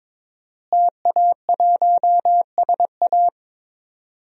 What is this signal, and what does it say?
Morse code, heard as a keyed tone, about 22 wpm, 700 Hz. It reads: TA1SA
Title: DXCC-100 Call Signs - 5 Characters - Repeated 2x - 22wpm